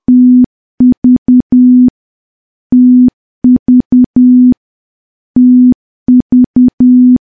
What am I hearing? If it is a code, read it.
Morse code, heard as a keyed tone, about 10 wpm, 250 Hz. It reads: TV TV TV
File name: morse / morse-training-10WPM